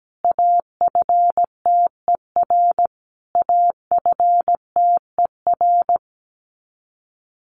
Morse code, heard as a keyed tone, about 17 wpm, 700 Hz. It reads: AFTER AFTER
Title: Top 200 Words Repeated 17wpm